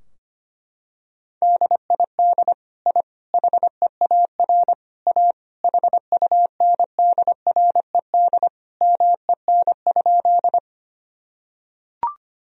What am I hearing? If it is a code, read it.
Morse code, heard as a keyed tone, about 25 wpm, 700 Hz. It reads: DID I HEAR A HUNDRED MEN?
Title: Sentences from Top 1000 Words - Review Part 2 - 25wpm